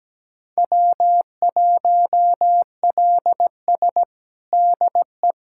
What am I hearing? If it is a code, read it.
Morse code, heard as a keyed tone, about 17 wpm, 700 Hz. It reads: W1LS DE
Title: USA Callsign DE Callsign - 4 Characters - 17wpm